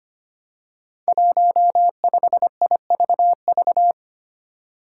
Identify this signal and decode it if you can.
Morse code, heard as a keyed tone, about 25 wpm, 700 Hz. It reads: 15IVV
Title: Five Challenging Letter-Number Combinations 25wpm